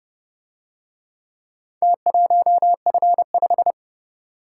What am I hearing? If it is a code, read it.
Morse code, heard as a keyed tone, about 30 wpm, 700 Hz. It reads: T1F5